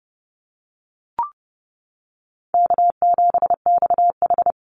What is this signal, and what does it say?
Morse code, heard as a keyed tone, about 30 wpm, 700 Hz. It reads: K7XH
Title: Call Signs - 30wpm